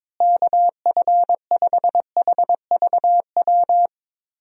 Morse code, heard as a keyed tone, about 22 wpm, 700 Hz. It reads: KF5HVW
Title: USA Call Signs - 6 Characters - Repeated 3x - 22wpm